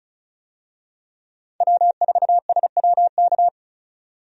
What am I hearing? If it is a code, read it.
Morse code, heard as a keyed tone, about 36 wpm, 700 Hz. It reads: W4SWK
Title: USA Call Signs - 5 Characters - 35wpm